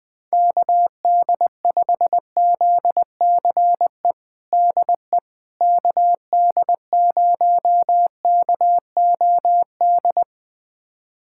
Morse code, heard as a keyed tone, about 21 wpm, 700 Hz. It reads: KD5ZCE DE KD0KOD
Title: USA Callsign DE Callsign - 6 Characters - 20wpm